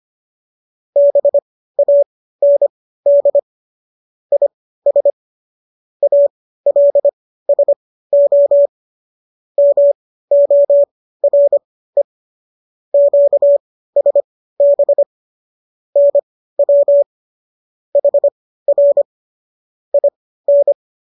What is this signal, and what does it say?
Morse code, heard as a keyed tone, about 25 wpm, spaced as for 15 wpm, 570 Hz. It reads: BAND IS ALSO MORE QSB NW HR IN